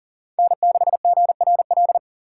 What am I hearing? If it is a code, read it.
Morse code, heard as a keyed tone, about 40 wpm, 700 Hz. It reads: N6GRL
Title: USA Call Signs - 5 Characters - 8x Character Spacing - 40wpm